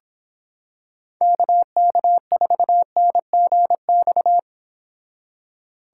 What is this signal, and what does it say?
Morse code, heard as a keyed tone, about 26 wpm, 700 Hz. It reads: KK4NGX